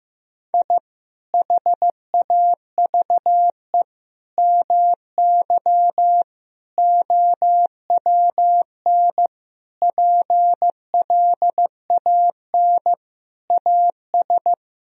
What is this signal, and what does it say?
Morse code, heard as a keyed tone, about 15 wpm, 700 Hz. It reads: I HAVE MY OWN PLAN AS
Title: Sentences from Top 1000 Words - Review Part 2 - 15wpm